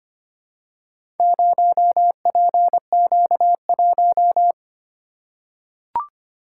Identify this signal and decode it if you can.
Morse code, heard as a keyed tone, about 25 wpm, 700 Hz. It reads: 0PQ1
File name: practice / 25wpm / Four Letter-Number Combinations - 25wpm